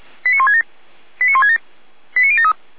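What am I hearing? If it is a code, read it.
Five-tone selective call (ZVEI), heard as quick runs of five tones, tones of about 70 ms. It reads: 89167 89167 89073
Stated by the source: ZVEI five-tone call received by SDR.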